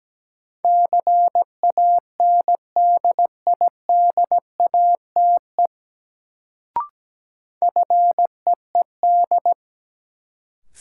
Morse code, heard as a keyed tone, about 17 wpm, 700 Hz. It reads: CANDIDATE FEED
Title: Sentences from Top 1600 Words - New Word Drill - 17wpm